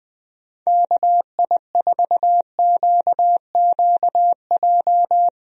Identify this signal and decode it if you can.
Morse code, heard as a keyed tone, about 20 wpm, 700 Hz. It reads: KI4QQJ
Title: USA Call Signs - 6 Characters - Repeated 3x - 20wpm